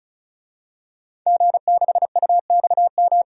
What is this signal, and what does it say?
Morse code, heard as a keyed tone, about 36 wpm, 700 Hz. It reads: G6UXM
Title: DXCC-100 Call Signs - 5 Characters - 35wpm